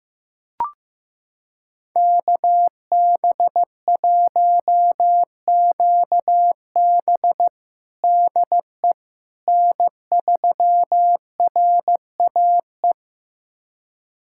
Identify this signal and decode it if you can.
Morse code, heard as a keyed tone, about 15 wpm, 700 Hz. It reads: KB1QB DE N3RAE
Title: USA Callsign DE Callsign - 5 Characters - 15wpm